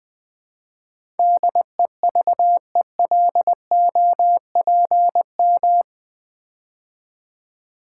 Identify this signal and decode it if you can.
Morse code, heard as a keyed tone, about 20 wpm, 700 Hz. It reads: DEVELOPM
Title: Build A Word - 4 Letters At A Time - 11-Letter Words - 20wpm